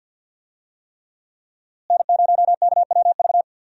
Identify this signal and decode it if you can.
Morse code, heard as a keyed tone, about 47 wpm, 700 Hz. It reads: N0KWV
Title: USA Call Signs - 5 Characters - Repeated 3x - 50wpm